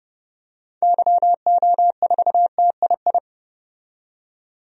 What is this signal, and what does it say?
Morse code, heard as a keyed tone, about 30 wpm, 700 Hz. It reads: YO4TII
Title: DXCC-100 Call Signs - 6 Characters - 30wpm